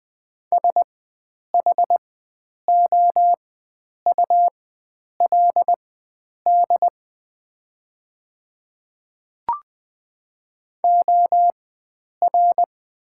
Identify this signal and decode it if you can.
Morse code, heard as a keyed tone, about 20 wpm, spaced as for 9 wpm, 690 Hz. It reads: SHOULD OR
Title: 4x Character Spacing - Top 300 Words - 20wpm